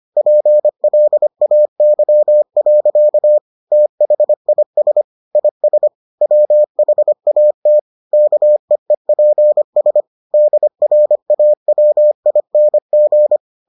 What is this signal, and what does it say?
Morse code, heard as a keyed tone, about 25 wpm, 600 Hz. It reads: PLAY. THIS IS WHAT KEEPS DRAWING